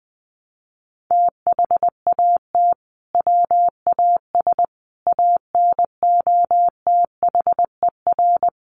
Morse code, heard as a keyed tone, about 20 wpm, 700 Hz. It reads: THAT WAS ANOTHER
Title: Sentences from Top 600 Words 20wpm